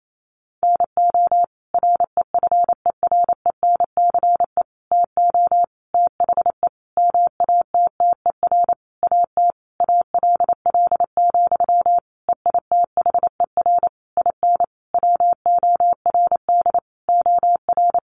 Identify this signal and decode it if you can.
Morse code, heard as a keyed tone, about 28 wpm, 700 Hz. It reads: NO REFERENCE TO THE MATTER AT ALL, EITHER IN WORD OR
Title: Sentences from Top 1200 Words 28wpm